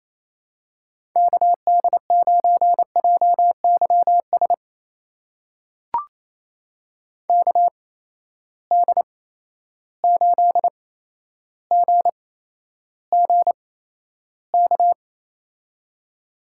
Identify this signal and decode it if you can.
Morse code, heard as a keyed tone, about 28 wpm, spaced as for 12 wpm, 700 Hz. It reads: KD9JYS KD8GGK
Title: USA Call Signs - 6 Characters - 8x Character Spacing - 28wpm